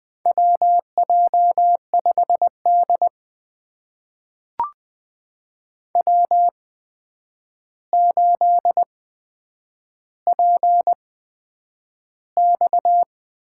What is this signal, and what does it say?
Morse code, heard as a keyed tone, about 20 wpm, spaced as for 9 wpm, 700 Hz. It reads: WJ5D W8PX
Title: USA Call Signs - 4 Characters - 8x Character Spacing - 20wpm